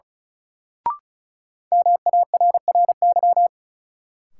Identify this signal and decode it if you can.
Morse code, heard as a keyed tone, about 34 wpm, 700 Hz. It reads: MARRY
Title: Sentences from Top 1200 Words - New Word Drill - 35wpm